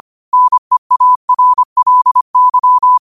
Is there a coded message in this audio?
Yes: Morse code from 0.3 to 3.0 s, heard as a keyed tone, about 25 wpm, 1000 Hz. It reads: NEARLY